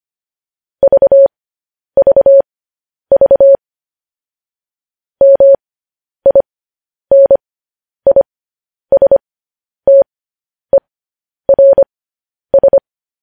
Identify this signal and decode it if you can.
Morse code, heard as a keyed tone, about 25 wpm, spaced as for 10 wpm, 550 Hz. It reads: VVV MINISTERS
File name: podcast / News headlines-1x-25x10